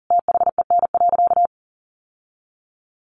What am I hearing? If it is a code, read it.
Morse code, heard as a keyed tone, about 40 wpm, 700 Hz. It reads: THEN.
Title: Sentences from Top 1000 Words 40wpm